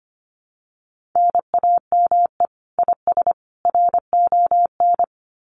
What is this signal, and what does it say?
Morse code, heard as a keyed tone, about 25 wpm, 700 Hz. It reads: NAME IS RON
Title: Common QSO Phrases 25wpm